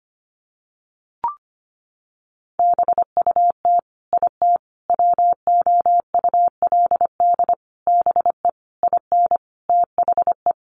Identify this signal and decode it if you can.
Morse code, heard as a keyed tone, about 25 wpm, 700 Hz. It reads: BUT IT WOULD BE IN THE
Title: Sentences from Top 300 Words 25wpm